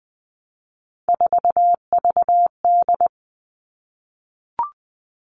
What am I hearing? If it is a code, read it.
Morse code, heard as a keyed tone, about 20 wpm, 700 Hz. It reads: 4VD
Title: All Cut Numbers - 3-Digit 20wpm